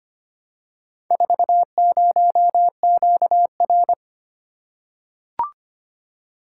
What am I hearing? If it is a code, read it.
Morse code, heard as a keyed tone, about 25 wpm, 700 Hz. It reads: 40QR
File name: practice / 25wpm / Four Letter-Number Combinations - 25wpm